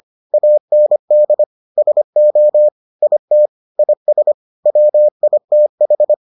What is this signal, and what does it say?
Morse code, heard as a keyed tone, about 25 wpm, 600 Hz. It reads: AND SO IT IS WITH